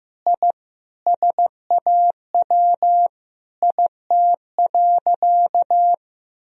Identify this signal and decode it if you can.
Morse code, heard as a keyed tone, about 14 wpm, 700 Hz. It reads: I SAW IT.
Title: Sentences from Top 1000 Words - Review Part 4 - 15wpm